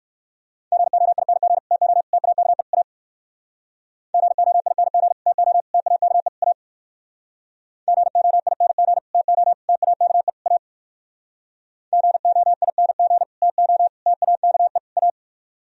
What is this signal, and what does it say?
Morse code, heard as a keyed tone, about 58 wpm, 700 Hz. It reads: GOING TO TAKE A GOING TO TAKE A GOING TO TAKE A GOING TO TAKE A